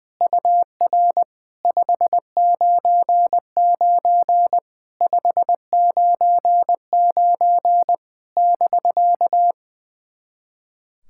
Morse code, heard as a keyed tone, about 20 wpm, 700 Hz. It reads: UR 599 599 <BK>
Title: SOTA QSO Practice 20wpm